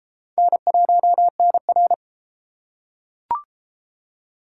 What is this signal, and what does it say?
Morse code, heard as a keyed tone, about 28 wpm, 700 Hz. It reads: N1NR